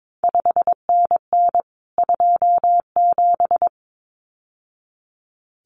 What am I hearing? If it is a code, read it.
Morse code, heard as a keyed tone, about 22 wpm, 700 Hz. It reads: 5NN 27
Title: CQ WW DX Contest Exchange Encore - 22wpm